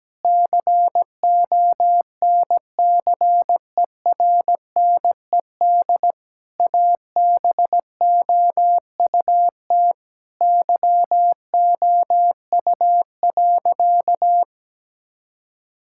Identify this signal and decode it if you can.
Morse code, heard as a keyed tone, about 17 wpm, 700 Hz. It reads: CONCERNED ABOUT YOU.
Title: Sentences from Top 1000 Words - Review Part 6 - 17wpm